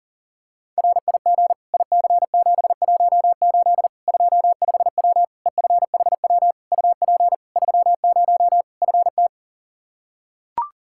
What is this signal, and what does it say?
Morse code, heard as a keyed tone, about 40 wpm, 700 Hz. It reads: RIG IC718 25W EFHW UP 30 FT
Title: Practice QSO - Exchange 2 - Info - 40wpm